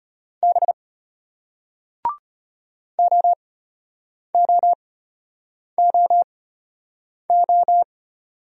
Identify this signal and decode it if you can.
Morse code, heard as a keyed tone, about 34 wpm, spaced as for 8 wpm, 700 Hz. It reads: B OOOO